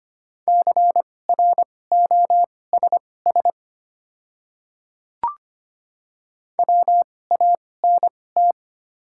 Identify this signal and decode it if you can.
Morse code, heard as a keyed tone, about 25 wpm, spaced as for 18 wpm, 690 Hz. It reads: CROSS WANT
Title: 2x Character Spacing - Top 300 Words - 25wpm